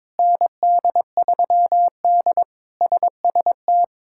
Morse code, heard as a keyed tone, about 22 wpm, 700 Hz. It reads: ND3D SST
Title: K1USN SST QSO Practice - 22wpm